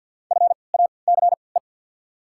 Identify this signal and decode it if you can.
Morse code, heard as a keyed tone, about 54 wpm, spaced as for 29 wpm, 690 Hz. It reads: FACE
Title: 3x Character Spacing - Top 500 Words - 50wpm